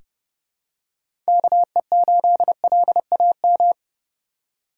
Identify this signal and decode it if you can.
Morse code, heard as a keyed tone, about 30 wpm, 700 Hz. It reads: KE8LAM
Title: USA Call Signs - 3x Character Spacing - 30wpm